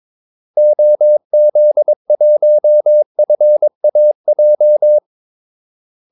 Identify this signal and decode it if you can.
Morse code, heard as a keyed tone, about 22 wpm, 600 Hz. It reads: OZ1FAJ
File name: Callsigns-22wpm